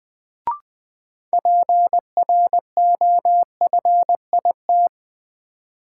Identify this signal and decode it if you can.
Morse code, heard as a keyed tone, about 20 wpm, 700 Hz. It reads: PROFIT
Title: Sentences from Top 800 Words - New Word Drill - 20wpm